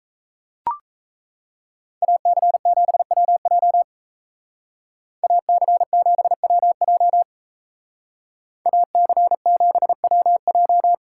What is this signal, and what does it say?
Morse code, heard as a keyed tone, about 35 wpm, 700 Hz. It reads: AC7WJ AC7WJ AC7WJ
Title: Speed Racing - USA 2x2 Call Signs 28wpm